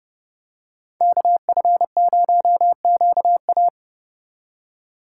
Morse code, heard as a keyed tone, about 29 wpm, 700 Hz. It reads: KF0QA
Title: USA Call Signs - 5 Characters - 3x Character Spacing - 30wpm